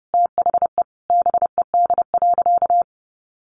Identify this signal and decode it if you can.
Morse code, heard as a keyed tone, about 30 wpm, 700 Hz. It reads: THE BED.
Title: Sentences from Top 1500 Words 30wpm